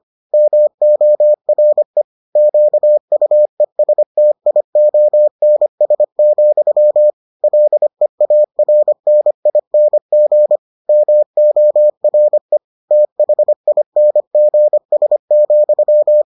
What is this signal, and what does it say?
Morse code, heard as a keyed tone, about 25 wpm, 600 Hz. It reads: MORE QUESTIONS, LEARNING MORE THINGS,